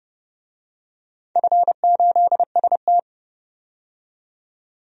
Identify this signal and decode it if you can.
Morse code, heard as a keyed tone, about 30 wpm, 700 Hz. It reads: F8ST
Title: DXCC-100 Call Signs - 4 Characters - 30wpm